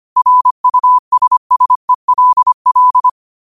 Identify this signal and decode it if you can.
Morse code, heard as a keyed tone, about 25 wpm, 1000 Hz. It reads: RUSSELL